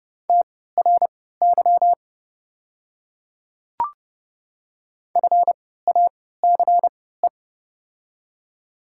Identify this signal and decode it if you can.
Morse code, heard as a keyed tone, about 32 wpm, spaced as for 17 wpm, 700 Hz. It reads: TRY FACE
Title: 3x Character Spacing - Top 300 Words - 30wpm